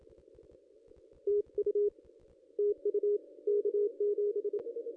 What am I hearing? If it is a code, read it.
Morse code, heard as a keyed tone, about 26 wpm, 410 Hz. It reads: TU TU K7
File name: contests / ss-cw-k5tr-op-n5rz -2200